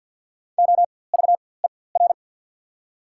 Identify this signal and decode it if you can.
Morse code, heard as a keyed tone, about 54 wpm, spaced as for 24 wpm, 690 Hz. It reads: OVER
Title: 4x Character Spacing - Top 100 Words - 50wpm